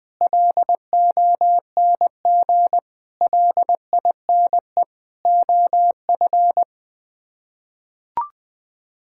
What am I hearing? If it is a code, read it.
Morse code, heard as a keyed tone, about 20 wpm, 700 Hz. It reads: LONG LINE OF
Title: Sets of 8 Words - 20wpm